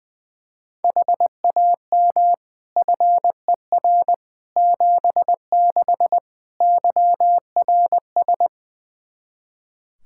Morse code, heard as a keyed tone, about 21 wpm, 700 Hz. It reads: HAM FER 76 YRS
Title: Practice QSO - Exchange 3 - More Info - 20wpm